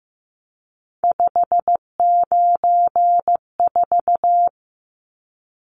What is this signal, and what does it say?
Morse code, heard as a keyed tone, about 15 wpm, 700 Hz. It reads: 594
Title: Numbers - 3-Digit 15wpm